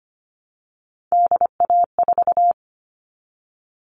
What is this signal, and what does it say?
Morse code, heard as a keyed tone, about 25 wpm, 700 Hz. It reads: DA4
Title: All Cut Numbers - 3-Digit 25wpm